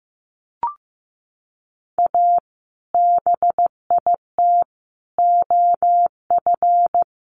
Morse code, heard as a keyed tone, about 15 wpm, 700 Hz. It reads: A BIT OF
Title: Sets of 4 Words 15wpm